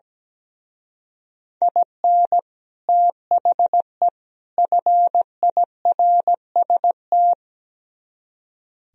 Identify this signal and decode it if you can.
Morse code, heard as a keyed tone, about 17 wpm, 700 Hz. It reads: IN THE FIRST
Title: Speed-Racing - Sets of 3 Words 17wpm